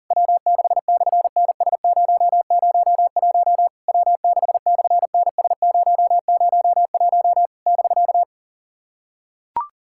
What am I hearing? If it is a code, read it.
Morse code, heard as a keyed tone, about 38 wpm, 690 Hz. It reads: W6/NS001 W6/NS001 <BK>
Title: SOTA QSO Practice 40wpm